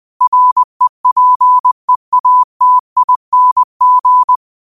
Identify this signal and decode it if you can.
Morse code, heard as a keyed tone, about 20 wpm, 1000 Hz. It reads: REPEATING
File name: 20wpm